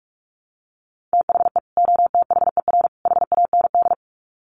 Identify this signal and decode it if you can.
Morse code, heard as a keyed tone, about 44 wpm, 700 Hz. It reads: THE OTHER HAND
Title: Sets of 3 Words 45wpm